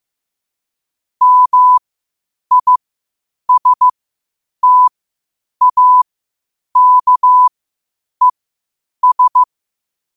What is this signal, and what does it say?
Morse code, heard as a keyed tone, about 15 wpm, spaced as for 8 wpm, 1000 Hz. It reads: MISTAKES
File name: qotdfiles / 8wpm